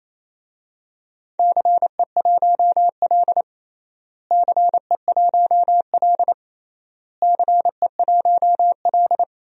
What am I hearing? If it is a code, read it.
Morse code, heard as a keyed tone, about 28 wpm, 700 Hz. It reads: CE1L CE1L CE1L
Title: DXCC-100 Call Signs - 4 Characters - Repeated 3x - 28wpm